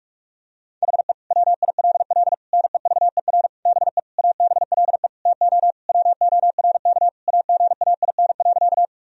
Morse code, heard as a keyed tone, about 41 wpm, 700 Hz. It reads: HE WILL NEVER BE ABLE TO WORK AGAIN.
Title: Sentences from Top 1000 Words - Review Part 2 - 45wpm